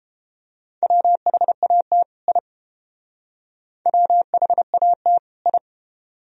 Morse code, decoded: WHAT I WHAT I

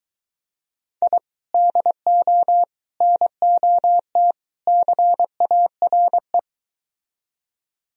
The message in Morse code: I DO NOT CARE

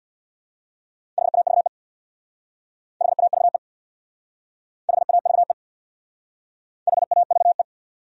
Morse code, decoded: HAVE HAVE HAVE HAVE

